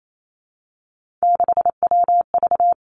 Morse code, heard as a keyed tone, about 28 wpm, 700 Hz. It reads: 6WV
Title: Three Challenging Letter-Number Combinations 28wpm